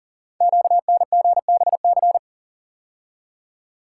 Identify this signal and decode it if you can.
Morse code, heard as a keyed tone, about 39 wpm, 700 Hz. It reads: QNGBC
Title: Code Groups - 5 Letters - 40wpm